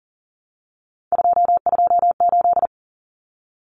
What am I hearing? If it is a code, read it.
Morse code, heard as a keyed tone, about 40 wpm, 700 Hz. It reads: 228
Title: Numbers - 3-Digit 40wpm